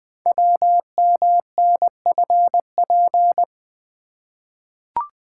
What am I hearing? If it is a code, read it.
Morse code, heard as a keyed tone, about 20 wpm, 700 Hz. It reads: WMNFP
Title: Code Groups - 5 Letters - 20wpm